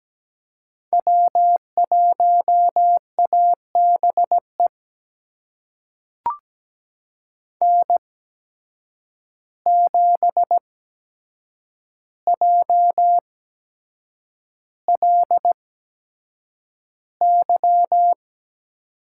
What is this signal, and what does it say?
Morse code, heard as a keyed tone, about 17 wpm, spaced as for 7 wpm, 700 Hz. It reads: W1ABE N7JLY
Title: USA Call Signs - 8x Character Spacing - 17wpm